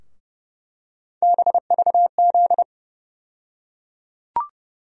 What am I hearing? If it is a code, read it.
Morse code, heard as a keyed tone, about 30 wpm, 700 Hz. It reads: BVZ
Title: USA Call Sign Postfixes - 3 Letters 30wpm